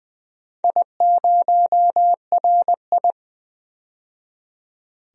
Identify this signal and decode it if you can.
Morse code, heard as a keyed tone, about 21 wpm, 700 Hz. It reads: I0RI
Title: DXCC-100 Call Signs - 4 Characters - 20wpm